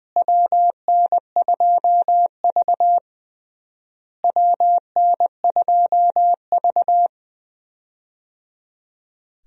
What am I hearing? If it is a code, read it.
Morse code, heard as a keyed tone, about 20 wpm, 700 Hz. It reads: WN2V WN2V
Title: USA Call Signs - 4 Characters - Repeated 3x - 20wpm